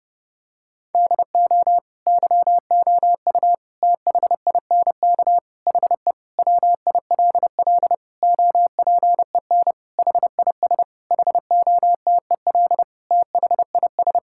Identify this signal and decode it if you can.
Morse code, heard as a keyed tone, about 32 wpm, 700 Hz. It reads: DO YOU THINK HE WILL OPEN HIS HOTEL THIS
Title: Sentences from Top 1600 Words 30wpm